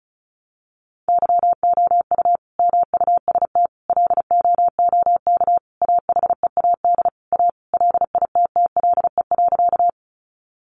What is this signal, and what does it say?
Morse code, heard as a keyed tone, about 34 wpm, 700 Hz. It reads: YOU MUST LOOK AHEAD A LITTLE.
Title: Sentences from Top 1100 Words 35wpm